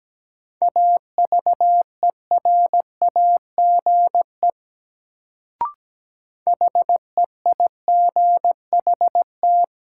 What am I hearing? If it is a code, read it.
Morse code, heard as a keyed tone, about 17 wpm, 700 Hz. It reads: AVERAGE HEIGHT
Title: Sets of 5 Words - New Word Drill - 17wpm